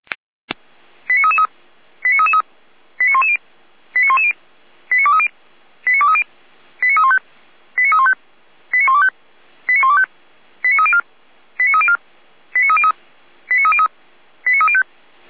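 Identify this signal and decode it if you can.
Five-tone selective call (ZVEI), heard as quick runs of five tones, tones of about 70 ms. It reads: 89333 89333 89110 89110 89230 89230 89315 89315 89125 89125 89304 89304 89303 89303 89305
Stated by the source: ZVEI five-tone selective call received on SDR, from the Feuerwehr Orlamünde audio server.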